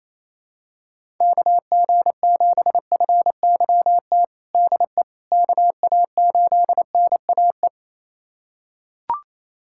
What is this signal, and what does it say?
Morse code, heard as a keyed tone, about 28 wpm, 700 Hz. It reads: KG7FYT DE KA8NAE
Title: USA Callsign DE Callsign - 6 Characters - 28wpm